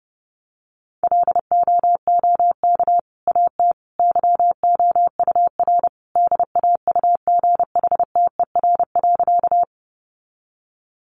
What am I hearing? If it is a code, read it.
Morse code, heard as a keyed tone, about 30 wpm, 700 Hz. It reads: LOOK AT YOUR DAUGHTER.
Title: Sentences from Top 1500 Words 30wpm